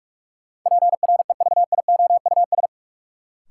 Morse code, heard as a keyed tone, about 46 wpm, 700 Hz. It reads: PREVIOUS
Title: Sentences from Top 1500 Words - New Word Drill - 45wpm